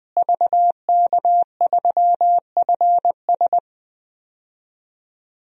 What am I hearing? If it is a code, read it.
Morse code, heard as a keyed tone, about 21 wpm, 700 Hz. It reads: VK3FS
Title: USA Call Signs - 5 Characters - Repeated 2x - 20wpm